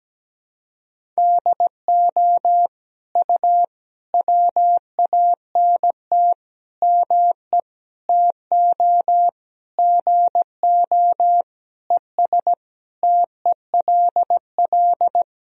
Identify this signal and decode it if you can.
Morse code, heard as a keyed tone, about 17 wpm, 700 Hz. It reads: DO U WANT ME TO GO ES TELL
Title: Sentences from Top 200 Words and Abbreviations 17wpm